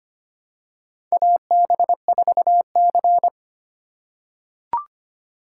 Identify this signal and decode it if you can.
Morse code, heard as a keyed tone, about 25 wpm, 700 Hz. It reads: AB4C
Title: USA Call Signs - 4 Characters - 4x Character Spacing - 25wpm